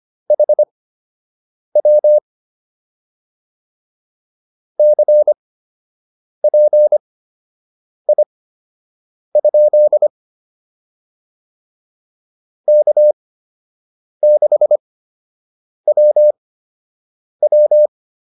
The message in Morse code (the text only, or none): HW CPI? K6WW